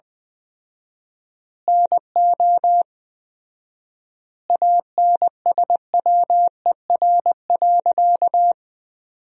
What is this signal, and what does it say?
Morse code, heard as a keyed tone, about 20 wpm, 700 Hz. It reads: NO ANSWER.
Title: 4x Word-Spacing - Sentences from Top 500 Words - 20wpm